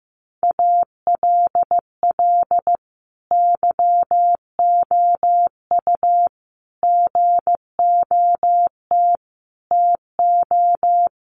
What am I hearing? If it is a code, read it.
Morse code, heard as a keyed tone, about 15 wpm, 700 Hz. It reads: ALL YOU GOT TO
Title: A Sound Of Thunder 15wpm